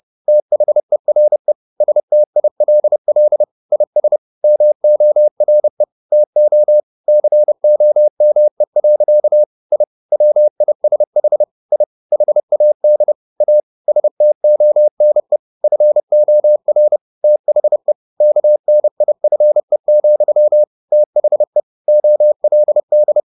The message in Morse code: THERE STILL IS MORE TO COME. I WISH I HAD A STONE FOR THE KNIFE, THE OLD

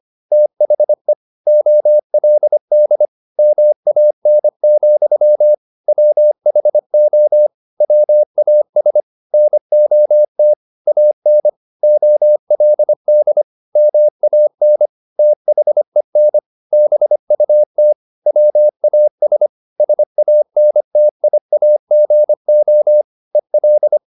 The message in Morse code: THE OLD MAN, WHO WAS NOT AN OLD MAN THEN BUT WAS SANTIAGO EL

